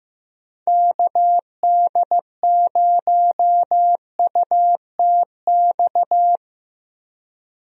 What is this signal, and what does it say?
Morse code, heard as a keyed tone, about 15 wpm, 700 Hz. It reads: KD0UTX
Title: USA Call Signs - 6 Characters - Repeated 2x - 15wpm